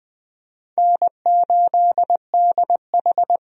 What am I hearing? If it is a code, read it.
Morse code, heard as a keyed tone, about 20 wpm, 700 Hz. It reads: N8DH
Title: USA Call Signs - 4 Characters - 20wpm